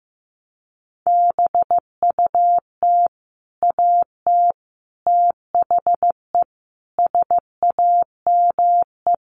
Morse code, heard as a keyed tone, about 15 wpm, 700 Hz. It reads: BUT AT THE SAME
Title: Sets of 4 Words 15wpm